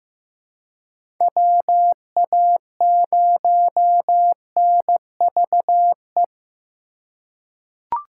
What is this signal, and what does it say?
Morse code, heard as a keyed tone, about 15 wpm, 700 Hz. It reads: WA0NVE
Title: USA Call Signs - 6 Characters - 2x Character Spacing - 15wpm